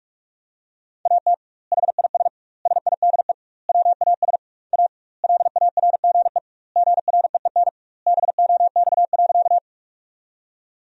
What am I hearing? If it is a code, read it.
Morse code, heard as a keyed tone, about 46 wpm, 700 Hz. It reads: AT HIS SIDE WAS A LARGE GREEN BOX.
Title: 2x Word-Spacing - Sentences from Top 500 Words - 45wpm